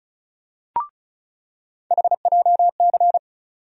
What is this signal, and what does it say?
Morse code, heard as a keyed tone, about 36 wpm, 700 Hz. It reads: HJC
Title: USA Call Sign Postfixes - 3 Letters 35wpm